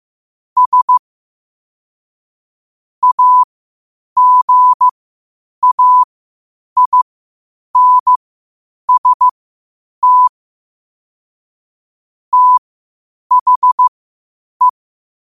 Morse code, heard as a keyed tone, about 15 wpm, spaced as for 8 wpm, 1000 Hz. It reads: S AGAINST THE